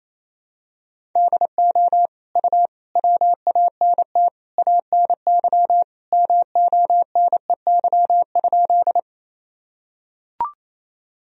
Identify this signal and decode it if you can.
Morse code, heard as a keyed tone, about 28 wpm, 700 Hz. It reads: DO U WANT ANY MONEY?
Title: Sentences from Top 400 Words and Abbreviations 28wpm